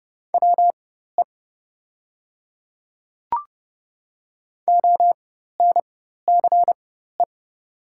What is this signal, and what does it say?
Morse code, heard as a keyed tone, about 28 wpm, spaced as for 14 wpm, 700 Hz. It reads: WE ONCE
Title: 4x Character Spacing - Top 300 Words - 30wpm